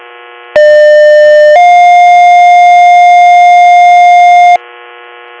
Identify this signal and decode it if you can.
Two-tone page, heard as a one-second tone, then a three-second tone, about 600 then 710 Hz. It is heard from 0.6 to 4.6 s.